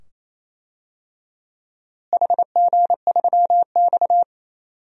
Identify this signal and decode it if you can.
Morse code, heard as a keyed tone, about 28 wpm, 700 Hz. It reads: HG3X
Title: DXCC-100 Call Signs - 4 Characters - Repeated 2x - 28wpm